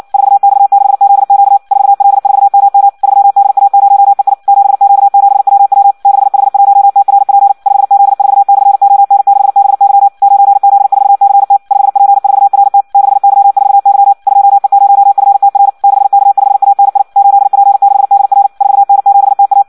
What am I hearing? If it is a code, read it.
Morse code, heard as a keyed tone, about 56 wpm, 800 Hz. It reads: 4X6ZQ 4L4MM 3DA0EI 7Q7WW 6H1TRP 5Z4/ON6BY 9Z4CT 5P4RT 6Y4Y 2E0FTR 6K5ANI 8P6DR 4N7TA
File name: Audio practica CW / qrq-tx-58 2